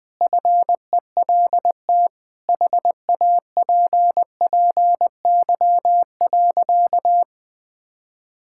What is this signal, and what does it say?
Morse code, heard as a keyed tone, about 20 wpm, 700 Hz. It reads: FELT HAPPY.
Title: Sentences with 8-Word Phrases - 20wpm